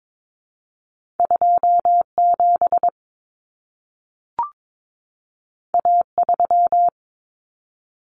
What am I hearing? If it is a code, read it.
Morse code, heard as a keyed tone, about 22 wpm, 700 Hz. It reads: 27 A3
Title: Common Cut Numbers - 2-Digit 22wpm